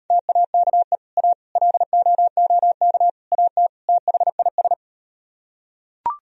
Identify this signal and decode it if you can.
Morse code, heard as a keyed tone, about 38 wpm, 700 Hz. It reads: TAKE A LOOK AT THIS